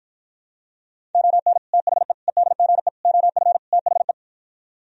Morse code, heard as a keyed tone, about 52 wpm, 700 Hz. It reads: ON THE EDGE OF THE